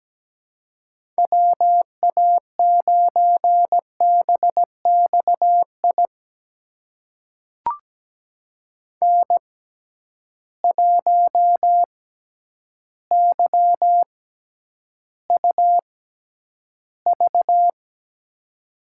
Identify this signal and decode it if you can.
Morse code, heard as a keyed tone, about 17 wpm, spaced as for 9 wpm, 700 Hz. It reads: WA9BXI N1YUV
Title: USA Call Signs - 6x Character Spacing - 17wpm